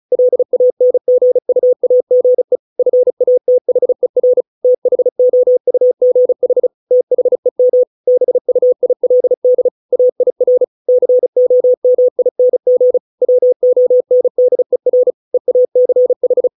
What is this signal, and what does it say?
Morse code, heard as a keyed tone, about 34 wpm, 500 Hz. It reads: LANGUAGE FATHER THOUGH THEM BUILD AIR COMING WONDER EACH